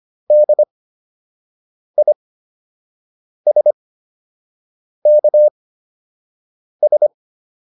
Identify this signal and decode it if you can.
Morse code, heard as a keyed tone, about 25 wpm, spaced as for 6 wpm, 600 Hz. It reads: DISKS